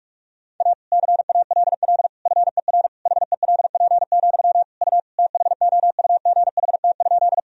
Morse code, heard as a keyed tone, about 46 wpm, 700 Hz. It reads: A CALL FER HELP, U THOUGHT?
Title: Sentences from Top 300 Words and Abbreviations 45wpm